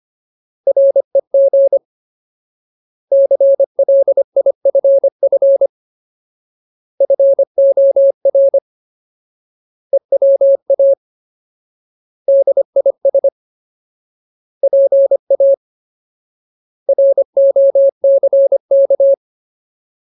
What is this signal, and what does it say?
Morse code, heard as a keyed tone, about 25 wpm, 550 Hz. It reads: REG CLIFF FOR EWA DIS PA ROCK
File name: Ultimate Word List / 25WPM Ultimate